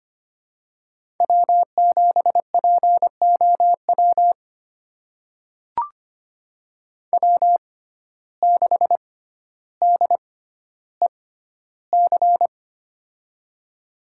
Morse code, heard as a keyed tone, about 25 wpm, spaced as for 13 wpm, 700 Hz. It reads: W7POW W6DEC
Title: USA Call Signs - 5 Characters - 6x Character Spacing - 25wpm